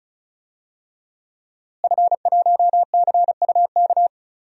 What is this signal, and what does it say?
Morse code, heard as a keyed tone, about 36 wpm, 700 Hz. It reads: F1CUK
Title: DXCC-100 Call Signs - 5 Characters - Repeated 3x - 35wpm